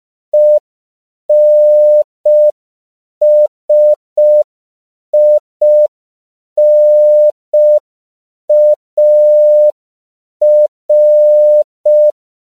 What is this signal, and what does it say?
Morse code, heard as a keyed tone, about 5 wpm, 600 Hz. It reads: ENSINAR
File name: CW / 5PPM